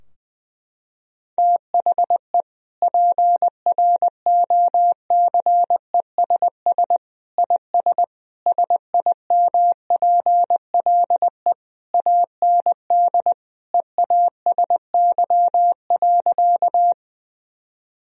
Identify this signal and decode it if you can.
Morse code, heard as a keyed tone, about 21 wpm, 700 Hz. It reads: THE PROCESS IS SIMPLE AND EASY.
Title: Sentences from Top 1000 Words - Review Part 3 - 20wpm